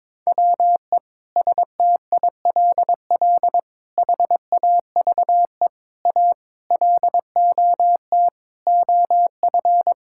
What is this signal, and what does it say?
Morse code, heard as a keyed tone, about 22 wpm, 700 Hz. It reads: WE STILL HAVE A LOT OF
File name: Sentences with 9-Word Phrases - 22wpm